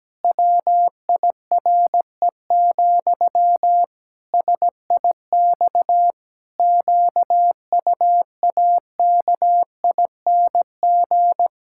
Morse code, heard as a keyed tone, about 17 wpm, 700 Hz. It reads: WIRE, SIX QUAKING
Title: Pangram Challenge v2 - 17wpm